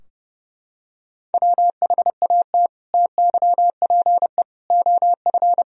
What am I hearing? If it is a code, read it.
Morse code, heard as a keyed tone, about 32 wpm, 700 Hz. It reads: WHAT TYPE OF
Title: Sentences from Top 1000 Words - Review Part 3 - 30wpm